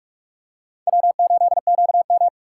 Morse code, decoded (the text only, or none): W8QM